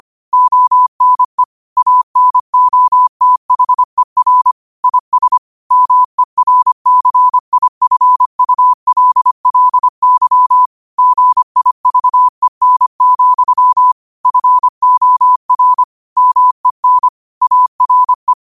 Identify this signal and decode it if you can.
Morse code, heard as a keyed tone, about 25 wpm, 1000 Hz. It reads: ONE ANOTHER IS MERCIFULLY GIVEN, FOR MEN ARE
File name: qotdfiles / 25wpm